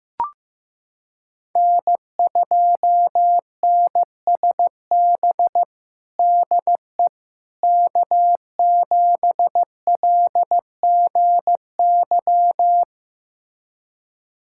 Morse code, heard as a keyed tone, about 15 wpm, 700 Hz. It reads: N2NSB DE K7LGY
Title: USA Callsign DE Callsign - 5 Characters - 15wpm